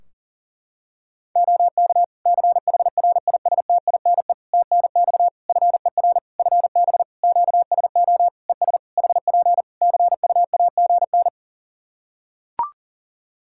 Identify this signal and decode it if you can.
Morse code, heard as a keyed tone, about 40 wpm, 700 Hz. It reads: OK CHRISTINE TNX FER FB QSO ES HP CUAGN
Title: Practice QSO - Exchange 4 - Ending - 40wpm